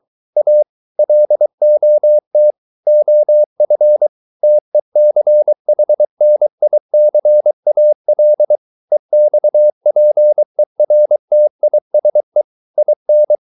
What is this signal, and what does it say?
Morse code, heard as a keyed tone, about 23 wpm, 600 Hz. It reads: A LOT OF TECHNICAL EXPERTISE IN